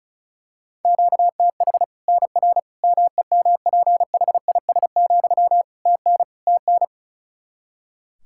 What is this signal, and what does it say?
Morse code, heard as a keyed tone, about 36 wpm, 700 Hz. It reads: QTH NR MEMPHIS, TN TN
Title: Practice QSO - Exchange 1 - Beginning - 35wpm